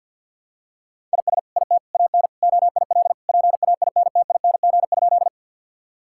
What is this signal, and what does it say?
Morse code, heard as a keyed tone, about 50 wpm, 700 Hz. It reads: IS IT AN OIL PAINTING?
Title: Sentences from Top 1000 Words - Review Part 6 - 50wpm